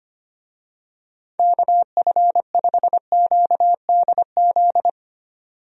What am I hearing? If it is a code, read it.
Morse code, heard as a keyed tone, about 25 wpm, 700 Hz. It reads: KF5QDZ
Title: USA Call Signs - 6 Characters - 25wpm